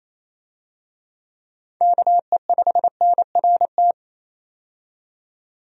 Morse code, heard as a keyed tone, about 28 wpm, 700 Hz. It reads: KE5NRT
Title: USA Call Signs - 6 Characters - 28wpm